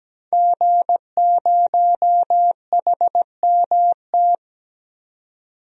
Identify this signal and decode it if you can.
Morse code, heard as a keyed tone, about 17 wpm, 700 Hz. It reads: G0HMT
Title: DXCC-100 Call Signs - 5 Characters - 17wpm